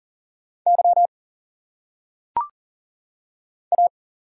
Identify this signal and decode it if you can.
Morse code, heard as a keyed tone, about 39 wpm, 700 Hz. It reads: Y A
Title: Single Letters - Phonetic Pronunciation 40wpm